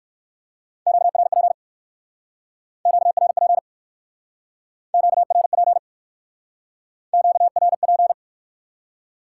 Morse code, heard as a keyed tone, about 55 wpm, 700 Hz. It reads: QRP QRP QRP QRP
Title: Speed-Racing Top 90 QSO Elements 45wpm